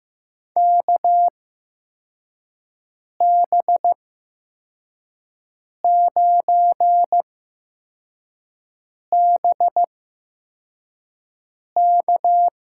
Morse code, heard as a keyed tone, about 15 wpm, spaced as for 4 wpm, 700 Hz. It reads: KB9BK
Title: USA Call Signs - 6 Characters - 8x Character Spacing - 15wpm